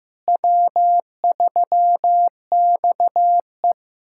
Morse code, heard as a keyed tone, about 15 wpm, 700 Hz. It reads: W3XE
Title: USA Callsign DE Callsign - 4 Characters - 15wpm